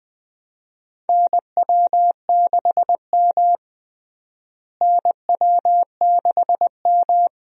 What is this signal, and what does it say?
Morse code, heard as a keyed tone, about 21 wpm, 700 Hz. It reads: NW6M NW6M
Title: USA Call Signs - 4 Characters - Repeated 3x - 20wpm